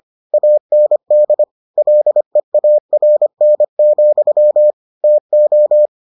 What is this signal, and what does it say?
Morse code, heard as a keyed tone, about 25 wpm, 600 Hz. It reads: AND LEARN, TO